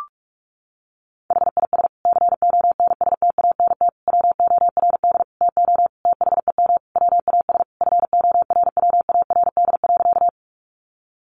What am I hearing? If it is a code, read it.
Morse code, heard as a keyed tone, about 44 wpm, 700 Hz. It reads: HIS CONSTANT WORD TO THEM WAS FORWARD.
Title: Sentences from Top 1500 Words 45wpm